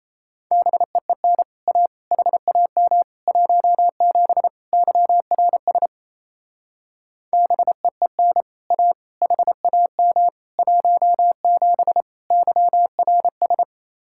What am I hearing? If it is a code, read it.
Morse code, heard as a keyed tone, about 33 wpm, 700 Hz. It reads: BEEN A HAM 17 YRS BEEN A HAM 17 YRS
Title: Speed-Racing Common QSO Phrases Encore - 28wpm